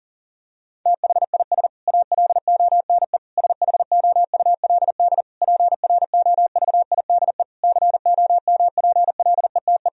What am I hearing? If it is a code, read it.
Morse code, heard as a keyed tone, about 40 wpm, 700 Hz. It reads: THIS ALONE SHOULD PROVIDE COMPLETE
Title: Sentences from Top 1000 Words - Review Part 4 - 40wpm